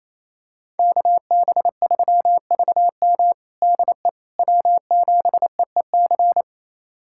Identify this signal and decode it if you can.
Morse code, heard as a keyed tone, about 28 wpm, 700 Hz. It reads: KB3VM DE W7EEC
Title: USA Callsign DE Callsign - 5 Characters - 28wpm